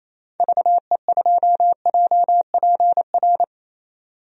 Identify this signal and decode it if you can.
Morse code, heard as a keyed tone, about 27 wpm, 700 Hz. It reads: VE2JPR
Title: DXCC-100 Call Signs - 6 Characters - 28wpm